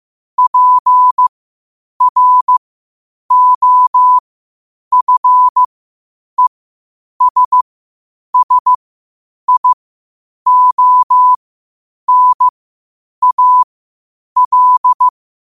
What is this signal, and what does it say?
Morse code, heard as a keyed tone, about 15 wpm, spaced as for 8 wpm, 1000 Hz. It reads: PROFESSIONAL
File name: qotdfiles / 8wpm